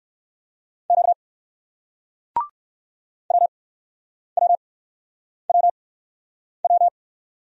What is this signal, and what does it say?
Morse code, heard as a keyed tone, about 60 wpm, spaced as for 9 wpm, 700 Hz. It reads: Q WWWW